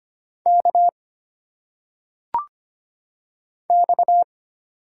Morse code, heard as a keyed tone, about 25 wpm, 700 Hz. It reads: K X
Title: Single Letters - Phonetic Pronunciation 25wpm